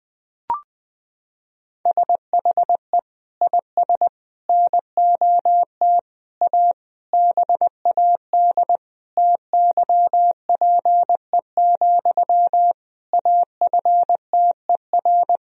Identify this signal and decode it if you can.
Morse code, heard as a keyed tone, about 21 wpm, 700 Hz. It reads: SHE IS NOT A BAD TYPE, AFTER
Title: Sentences from Top 1000 Words - Review Part 3 - 20wpm